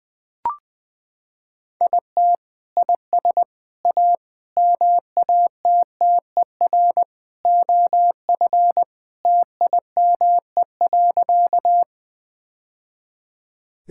Morse code, heard as a keyed tone, about 20 wpm, 700 Hz. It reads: IT IS A MATTER OF TIME.
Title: Sentences with 3-Word Phrases - 20wpm